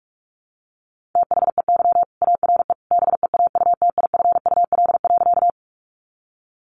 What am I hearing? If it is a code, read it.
Morse code, heard as a keyed tone, about 44 wpm, 700 Hz. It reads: THEY ARE BEAUTIFUL.
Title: Sentences from Top 900 Words 45wpm